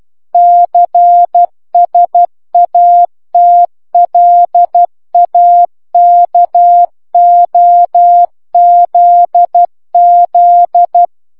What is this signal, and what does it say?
Morse code, heard as a keyed tone, about 12 wpm, 700 Hz. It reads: CSATLAKOZZ